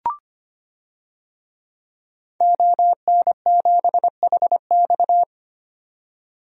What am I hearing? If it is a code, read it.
Morse code, heard as a keyed tone, about 25 wpm, 700 Hz. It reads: ON7HX
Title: DXCC-100 Call Signs - 5 Characters - 25wpm